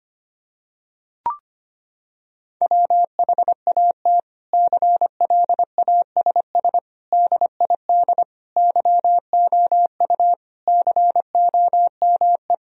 Morse code, heard as a keyed tone, about 25 wpm, 700 Hz. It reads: WHAT CLASS DID YOU COME
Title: Sentences from Top 1000 Words - Review Part 2 - 25wpm